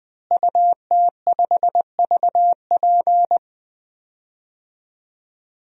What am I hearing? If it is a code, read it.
Morse code, heard as a keyed tone, about 20 wpm, 700 Hz. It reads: UT5VP
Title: DXCC-100 Call Signs - 5 Characters - Repeated 3x - 20wpm